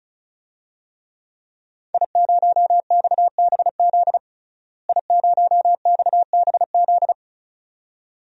Morse code, heard as a keyed tone, about 36 wpm, 700 Hz. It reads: I0XBZ I0XBZ
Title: DXCC-100 Call Signs - 5 Characters - Repeated 2x - 35wpm